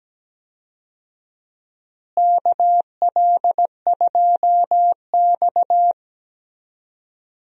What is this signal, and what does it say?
Morse code, heard as a keyed tone, about 17 wpm, 700 Hz. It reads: KL2X
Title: USA Call Signs - 4 Characters - Repeated 3x - 17wpm